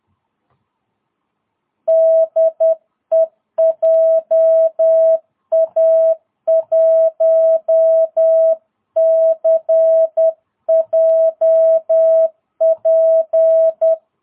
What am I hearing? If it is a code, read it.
Morse code, heard as a keyed tone, about 10 wpm, 660 Hz. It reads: DE JA1CJP
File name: jh8cbh / 2022cw